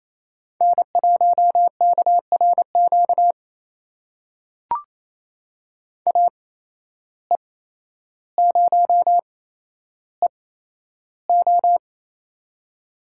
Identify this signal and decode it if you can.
Morse code, heard as a keyed tone, about 28 wpm, spaced as for 12 wpm, 700 Hz. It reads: N1KRQ AE0EO
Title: USA Call Signs - 5 Characters - 8x Character Spacing - 28wpm